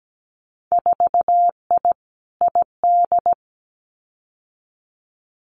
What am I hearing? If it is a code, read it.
Morse code, heard as a keyed tone, about 17 wpm, 700 Hz. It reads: 4I ID
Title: Winter Field Day Exchanges 17wpm